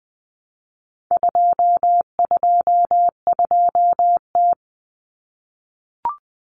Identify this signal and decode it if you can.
Morse code, heard as a keyed tone, about 20 wpm, 700 Hz. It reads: 222T
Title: Common Cut Numbers - 4-Digit 20wpm